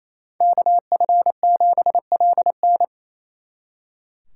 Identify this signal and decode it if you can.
Morse code, heard as a keyed tone, about 33 wpm, 700 Hz. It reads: KF7LN
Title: Speed Racing - USA 2x2 Call Signs 28wpm